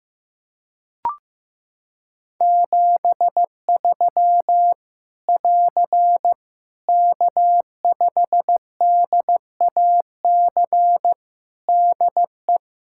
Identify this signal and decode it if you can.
Morse code, heard as a keyed tone, about 15 wpm, 700 Hz. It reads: 73 <AR> K5DAC DE
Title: Practice QSO - Exchange 4 - Ending - 15wpm